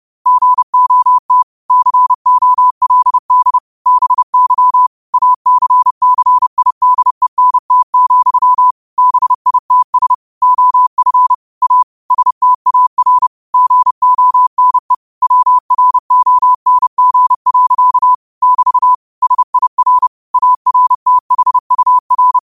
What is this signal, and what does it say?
Morse code, heard as a keyed tone, about 30 wpm, 1000 Hz. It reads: GOT COLD BY ACCIDENT, BITS OF A STAR GONE WRONG. <BT> SIR ARTHUR